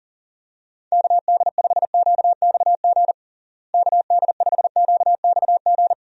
Morse code, heard as a keyed tone, about 40 wpm, 700 Hz. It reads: KD5QXG KD5QXG
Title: USA Call Signs - 6 Characters - Repeated 3x - 40wpm